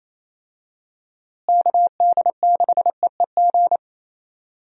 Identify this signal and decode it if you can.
Morse code, heard as a keyed tone, about 27 wpm, 700 Hz. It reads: KD6EEG
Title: USA Call Signs - 6 Characters - Repeated 2x - 28wpm